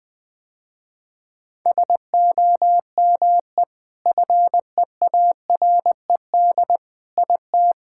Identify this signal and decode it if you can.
Morse code, heard as a keyed tone, about 20 wpm, 700 Hz. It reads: SOME FEARED IT
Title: Sentences with 5-Word Phrases - Spoken Beginning - 20wpm